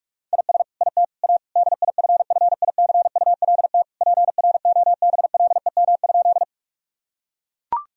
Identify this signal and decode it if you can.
Morse code, heard as a keyed tone, about 41 wpm, 700 Hz. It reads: IS IT A DIFFICULT PROBLEM?
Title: Sentences from Top 1000 Words - Review Part 2 - 45wpm